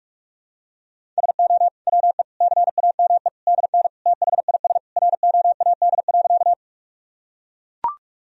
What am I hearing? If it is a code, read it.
Morse code, heard as a keyed tone, about 41 wpm, 700 Hz. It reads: SO WE CAME DN THIS ROAD.
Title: Sentences from Top 400 Words and Abbreviations 45wpm